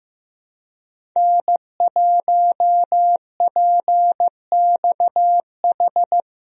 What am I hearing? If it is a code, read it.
Morse code, heard as a keyed tone, about 15 wpm, 700 Hz. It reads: N1PXH
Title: USA Call Signs - 5 Characters - Repeated 3x - 15wpm